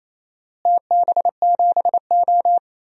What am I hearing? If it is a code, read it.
Morse code, heard as a keyed tone, about 27 wpm, 700 Hz. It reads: TB7O
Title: DXCC-100 Call Signs - 4 Characters - Repeated 2x - 28wpm